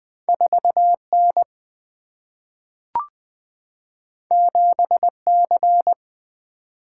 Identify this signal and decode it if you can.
Morse code, heard as a keyed tone, about 20 wpm, 700 Hz. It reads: 4N 7C
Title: Single-Letter Single-Number 20wpm